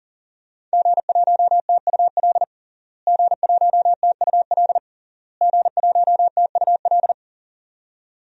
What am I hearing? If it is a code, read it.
Morse code, heard as a keyed tone, about 40 wpm, 700 Hz. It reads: G1TUL G1TUL G1TUL
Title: DXCC-100 Call Signs - 5 Characters - Repeated 3x - 40wpm